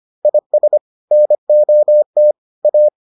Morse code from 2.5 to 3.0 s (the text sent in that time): A